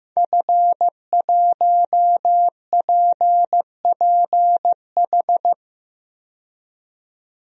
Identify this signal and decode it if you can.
Morse code, heard as a keyed tone, about 15 wpm, 700 Hz. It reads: F1PPH
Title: DXCC-100 Call Signs - 5 Characters - 15wpm